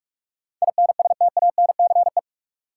Morse code, heard as a keyed tone, about 46 wpm, 700 Hz. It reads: INSTANCE
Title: Sentences from Top 800 Words - New Word Drill - 45wpm